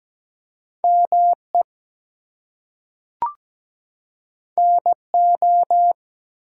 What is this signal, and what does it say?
Morse code, heard as a keyed tone, about 17 wpm, 700 Hz. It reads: ME NO
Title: Two Letter Words - 17wpm